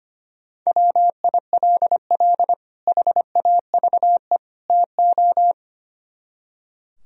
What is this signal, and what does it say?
Morse code, heard as a keyed tone, about 32 wpm, 700 Hz. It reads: WILL HAVE TO